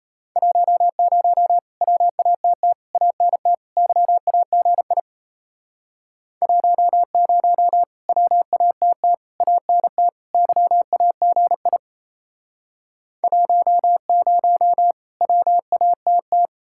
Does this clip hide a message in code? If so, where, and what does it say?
Yes: Morse code from 0.4 to 16.5 s, heard as a keyed tone, about 36 wpm, 700 Hz. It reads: 10 WATT ANT YAGI 10 WATT ANT YAGI 10 WATT